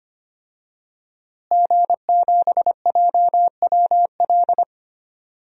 Morse code, heard as a keyed tone, about 25 wpm, 700 Hz. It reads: G7JWL
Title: DXCC-100 Call Signs - 5 Characters - Repeated 3x - 25wpm